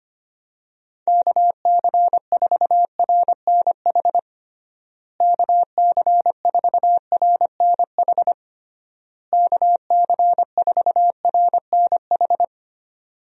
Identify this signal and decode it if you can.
Morse code, heard as a keyed tone, about 25 wpm, 700 Hz. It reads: KC4RNH KC4RNH KC4RNH
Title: USA Call Signs - 6 Characters - Repeated 3x - 25wpm